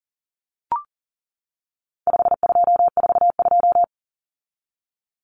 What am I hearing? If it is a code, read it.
Morse code, heard as a keyed tone, about 40 wpm, 700 Hz. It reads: 5242
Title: Numbers - 4-Digit 40wpm